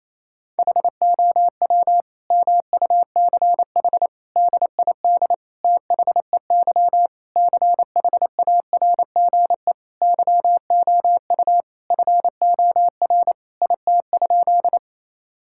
Morse code, heard as a keyed tone, about 28 wpm, 700 Hz. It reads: HOW MUCH DID THEY CHARGE YOU FOR IT?
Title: Sentences from Top 1000 Words - Review Part 6 - 28wpm